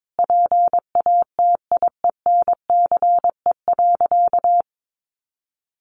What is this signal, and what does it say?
Morse code, heard as a keyed tone, about 22 wpm, 700 Hz. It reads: PATIENCE.
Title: Sentences from Top 1400 Words 22wpm